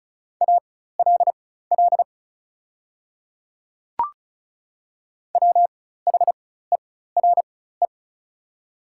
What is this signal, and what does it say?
Morse code, heard as a keyed tone, about 36 wpm, spaced as for 16 wpm, 690 Hz. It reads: ALL WHERE
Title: 4x Character Spacing - Top 300 Words - 35wpm